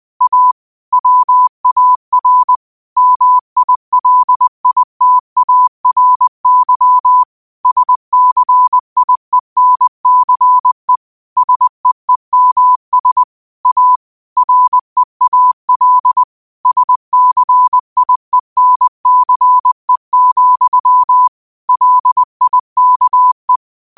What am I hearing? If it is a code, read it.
Morse code, heard as a keyed tone, about 20 wpm, 1000 Hz. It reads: A WAR MILITARY SCIENCE SEEMS A REAL SCIENCE, LIKE